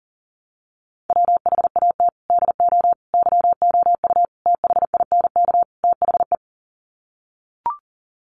Morse code, heard as a keyed tone, about 40 wpm, 700 Hz. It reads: WHAT DO YOU THINK THE
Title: Sets of 5 Words Encore 40wpm